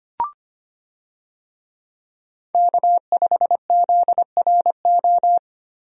Morse code, heard as a keyed tone, about 25 wpm, 700 Hz. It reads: K5ZRO
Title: USA Call Signs - 5 Characters - Repeated 3x - 25wpm